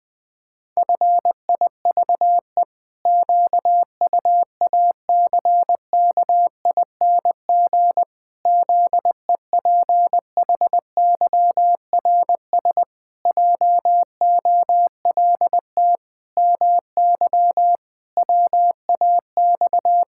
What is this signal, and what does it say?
Morse code, heard as a keyed tone, about 20 wpm, 700 Hz. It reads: FIVE QUACKING ZEPHYRS JOLT MY WAX
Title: Pangram Challenge v2 - 20wpm